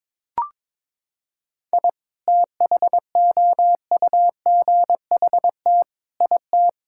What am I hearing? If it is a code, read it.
Morse code, heard as a keyed tone, about 22 wpm, 700 Hz. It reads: I THOUGHT IT
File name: Sentences with 9-Word Phrases - 22wpm